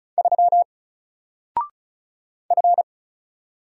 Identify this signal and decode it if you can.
Morse code, heard as a keyed tone, about 35 wpm, 700 Hz. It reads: 3 F
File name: Single-Letter-Number-35wpm